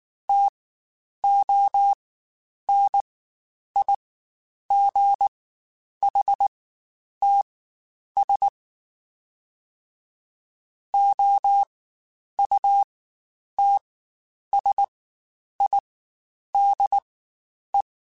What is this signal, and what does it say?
Morse code, heard as a keyed tone, about 19 wpm, spaced as for 9 wpm, 800 Hz. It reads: TONIGHTS OUTSIDE
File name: Spaced Characters / Spaced Characters-mp3s / Weather 4 10 sp @18